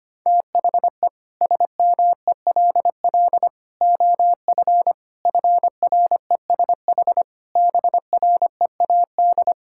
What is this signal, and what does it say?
Morse code, heard as a keyed tone, about 25 wpm, 700 Hz. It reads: THE SMELL OF FRESH BREAD
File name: practice / 25wpm / Difficult Sentences with Hints - 25wpm